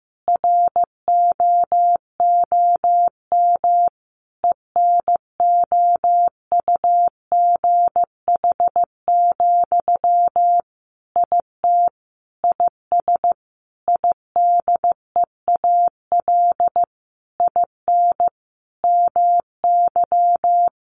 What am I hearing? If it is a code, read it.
Morse code, heard as a keyed tone, about 15 wpm, 700 Hz. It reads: ROOM ENOUGH, IT IS IDEAL IN MY
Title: Sentences from Top 1400 Words 15wpm